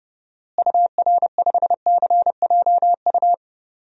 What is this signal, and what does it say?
Morse code, heard as a keyed tone, about 29 wpm, 700 Hz. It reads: UR5CJU
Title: DXCC-100 Call Signs - 6 Characters - Repeated 2x - 30wpm